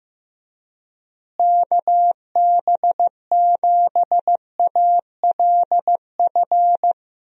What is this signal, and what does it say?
Morse code, heard as a keyed tone, about 15 wpm, 700 Hz. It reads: KB7ALF
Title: USA Call Signs - 6 Characters - Repeated 3x - 15wpm